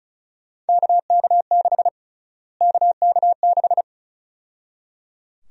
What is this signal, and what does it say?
Morse code, heard as a keyed tone, about 36 wpm, 700 Hz. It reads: KK6 KK6
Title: USA Call Sign Prefixes - Repeated 2x - 35wpm